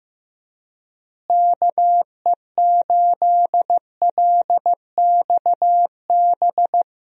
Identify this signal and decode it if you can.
Morse code, heard as a keyed tone, about 15 wpm, 700 Hz. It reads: KE8LXB
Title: USA Call Signs - 6 Characters - Repeated 2x - 15wpm